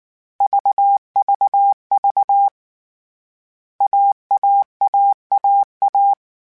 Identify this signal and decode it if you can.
Morse code, heard as a keyed tone, about 19 wpm, 800 Hz. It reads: VVV AAAAA
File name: CW Code Groups-mp3s / Sets mp3s / AUV AUV4 sp @18